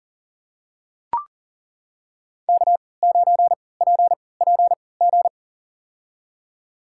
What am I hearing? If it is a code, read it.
Morse code, heard as a keyed tone, about 40 wpm, spaced as for 23 wpm, 700 Hz. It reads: K9PPG
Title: USA Call Signs - 5 Characters - 3x Character Spacing - 40wpm